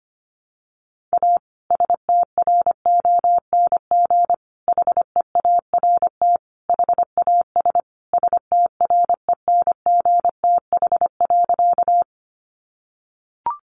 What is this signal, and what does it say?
Morse code, heard as a keyed tone, about 25 wpm, 700 Hz. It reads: A STRONG HEART HAS STRENGTH.
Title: Sentences from Top 1300 Words 25wpm